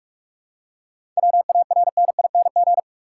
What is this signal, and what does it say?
Morse code, heard as a keyed tone, about 46 wpm, 700 Hz. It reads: WARNING
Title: Seven Letter Words - 45wpm